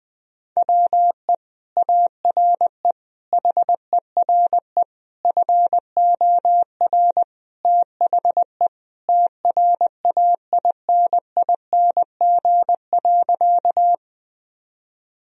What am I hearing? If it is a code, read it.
Morse code, heard as a keyed tone, about 20 wpm, 700 Hz. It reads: WE ARE HERE FOR THE TRAINING.
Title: Sentences from Top 1000 Words - Review Part 4 - 20wpm